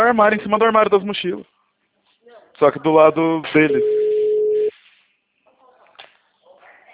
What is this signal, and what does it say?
Ringback tone (Europe, 425 Hz), with speech.
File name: out- -2013- - - .2